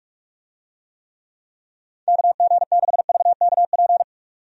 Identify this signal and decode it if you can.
Morse code, heard as a keyed tone, about 46 wpm, 700 Hz. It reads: KG6VKP
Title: USA Call Signs - 6 Characters - 45wpm